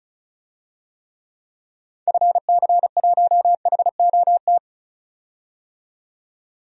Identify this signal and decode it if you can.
Morse code, heard as a keyed tone, about 34 wpm, 700 Hz. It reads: FC1HOT
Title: DXCC-100 Call Signs - 6 Characters - 35wpm